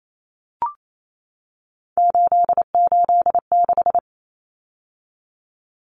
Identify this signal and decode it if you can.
Morse code, heard as a keyed tone, about 28 wpm, 700 Hz. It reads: 886
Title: Numbers - 3-Digit 28wpm